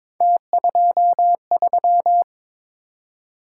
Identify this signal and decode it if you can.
Morse code, heard as a keyed tone, about 23 wpm, 700 Hz. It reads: T23
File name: practice / 22wpm / Three Letter-Number Combinations - 22wpm